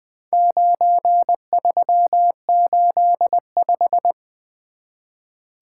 Morse code, heard as a keyed tone, about 20 wpm, 700 Hz. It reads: 9385
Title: Four Letter-Number Combinations - 20wpm